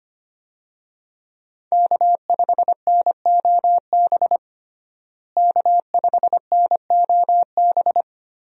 Morse code, heard as a keyed tone, about 25 wpm, 700 Hz. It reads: K5NOB K5NOB
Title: USA Call Signs - 5 Characters - Repeated 3x - 25wpm